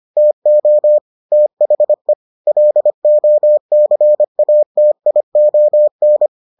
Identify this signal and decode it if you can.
Morse code, heard as a keyed tone, about 25 wpm, 600 Hz. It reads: TO THE LOCATION